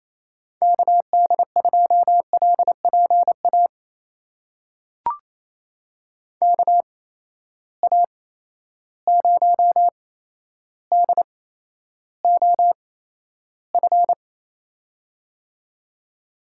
Morse code, heard as a keyed tone, about 28 wpm, spaced as for 12 wpm, 700 Hz. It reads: KD2LPA KA0DOF
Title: USA Call Signs - 6 Characters - 8x Character Spacing - 28wpm